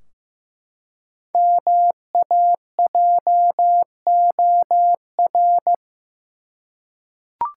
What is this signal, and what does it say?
Morse code, heard as a keyed tone, about 15 wpm, 700 Hz. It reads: MAJOR
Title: Five Letter Words - 15wpm